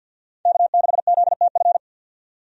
Morse code, heard as a keyed tone, about 50 wpm, 700 Hz. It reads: K6ZTF
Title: USA Call Signs - 5 Characters - 50wpm